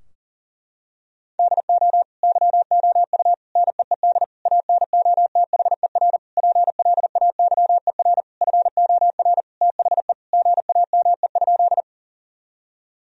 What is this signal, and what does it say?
Morse code, heard as a keyed tone, about 40 wpm, 700 Hz. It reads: DO YOU NEED ANOTHER PLAYER FOR THE GAME?
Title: Sentences from Top 1000 Words - Review Part 4 - 40wpm